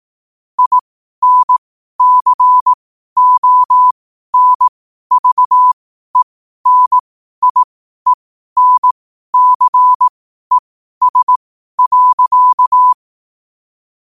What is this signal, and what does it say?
Morse code, heard as a keyed tone, about 18 wpm, spaced as for 12 wpm, 1000 Hz. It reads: INCONVENIENCES.
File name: qotdfiles / 12wpm